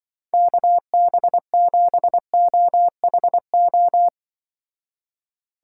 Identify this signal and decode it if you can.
Morse code, heard as a keyed tone, about 24 wpm, 700 Hz. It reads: KB7OHO
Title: Speed Racing - USA 2x3 Call Signs 20wpm